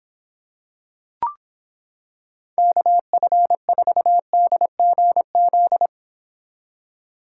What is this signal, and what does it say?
Morse code, heard as a keyed tone, about 26 wpm, 700 Hz. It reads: KF4DGZ